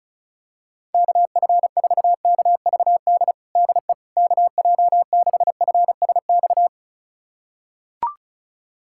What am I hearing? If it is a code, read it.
Morse code, heard as a keyed tone, about 36 wpm, 700 Hz. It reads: KF4KVD DE KJ6FSX
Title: USA Callsign DE Callsign - 6 Characters - 35wpm